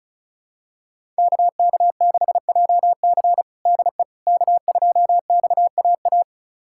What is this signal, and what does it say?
Morse code, heard as a keyed tone, about 36 wpm, 700 Hz. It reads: KK6JC DE K2XAA
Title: USA Callsign DE Callsign - 5 Characters - 35wpm